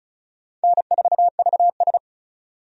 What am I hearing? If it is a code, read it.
Morse code, heard as a keyed tone, about 36 wpm, 700 Hz. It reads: N4VS
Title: USA Call Signs - 4 Characters - 35wpm